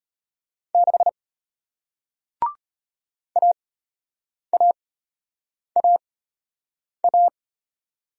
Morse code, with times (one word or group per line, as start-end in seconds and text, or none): 0.8-1.1 6
3.4-7.3 AAAA